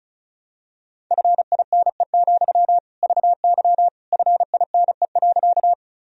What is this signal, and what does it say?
Morse code, heard as a keyed tone, about 36 wpm, 700 Hz. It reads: FINE, VY FINE.
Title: Sentences from Top 500 Words and Abbreviations 35wpm